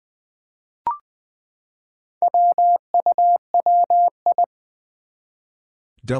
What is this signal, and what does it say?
Morse code, heard as a keyed tone, about 20 wpm, 700 Hz. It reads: WUWI
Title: Four Challenging Letter-Number Combinations 20wpm